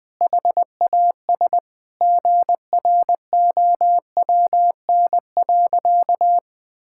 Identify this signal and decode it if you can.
Morse code, heard as a keyed tone, about 21 wpm, 700 Hz. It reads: HAS GROWN.
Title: Sentences with 4-Word Phrases - 20wpm